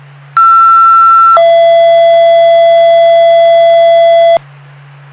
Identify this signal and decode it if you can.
Two-tone page, heard as a one-second tone, then a three-second tone, about 1340 then 670 Hz, with noise. It is heard from 0.4 to 4.4 s.